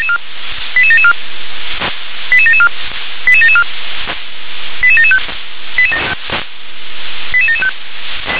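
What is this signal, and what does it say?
Five-tone selective call (ZVEI), heard as quick runs of five tones, tones of about 70 ms. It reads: 88774 88774 88774 88775 88775
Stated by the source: ZVEI five-tone selective call, code 88771, received by SDR.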